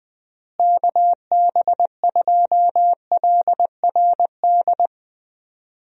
Morse code, heard as a keyed tone, about 21 wpm, 700 Hz. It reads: KB2LRD
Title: USA Call Signs - 6 Characters - Repeated 3x - 20wpm